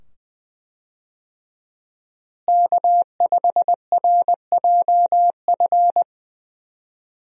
Morse code, heard as a keyed tone, about 20 wpm, 700 Hz. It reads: K5RJF
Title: USA Call Signs - 5 Characters - Repeated 2x - 20wpm